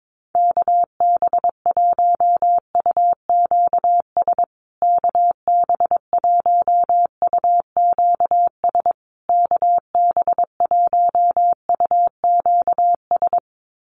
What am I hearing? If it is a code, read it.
Morse code, heard as a keyed tone, about 22 wpm, 700 Hz. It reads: KB1UQS KB1UQS KB1UQS
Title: USA Call Signs Repeated 3x 22wpm